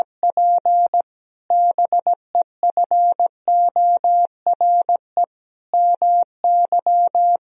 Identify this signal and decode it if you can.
Morse code, heard as a keyed tone, about 17 wpm, 700 Hz. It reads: P BEFORE MY
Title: Sentences from Top 1000 Words - Review Part 1 - 17wpm